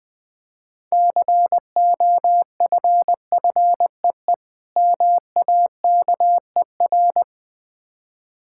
Morse code, heard as a keyed tone, about 20 wpm, 700 Hz. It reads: COFFEE MAKER
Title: Difficult Words - Speed-Racing with Hints - 20wpm